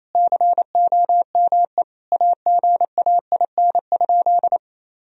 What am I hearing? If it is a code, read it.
Morse code, heard as a keyed tone, about 28 wpm, 700 Hz. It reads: COME AGAIN?
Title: Sentences from Top 1000 Words - Review Part 4 - 28wpm